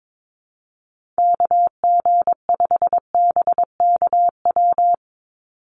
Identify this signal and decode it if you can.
Morse code, heard as a keyed tone, about 22 wpm, 700 Hz. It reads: KG5BKW
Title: USA Call Signs 22wpm